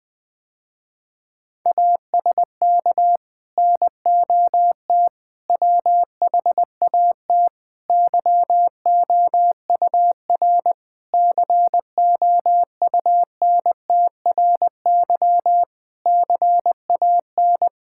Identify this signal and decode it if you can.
Morse code, heard as a keyed tone, about 20 wpm, 700 Hz. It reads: ASK NOT WHAT YOUR COUNTRY CAN
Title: Sentences with 9-Word Phrases - Spoken Beginning - 20wpm